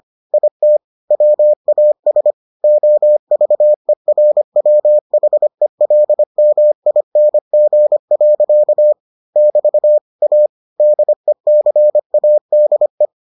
Morse code, heard as a keyed tone, about 25 wpm, 600 Hz. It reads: IT WAS OVERWHELMING. <BT> A DECADE